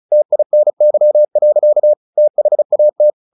Morse code, heard as a keyed tone, about 34 wpm, 600 Hz. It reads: TINY. THAT
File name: the old man and the sea 2 35wpm